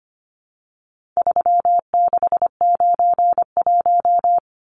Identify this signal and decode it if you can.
Morse code, heard as a keyed tone, about 25 wpm, 700 Hz. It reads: 3691
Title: Numbers - 4-Digit 25wpm